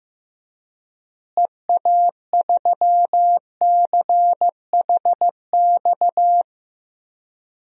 Morse code, heard as a keyed tone, about 15 wpm, 700 Hz. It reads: EA3CHX
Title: DXCC-100 Call Signs - 6 Characters - Repeated 2x - 15wpm